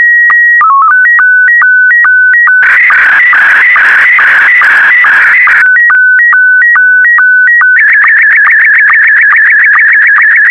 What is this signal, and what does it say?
SSTV (slow-scan television), Scottie 1.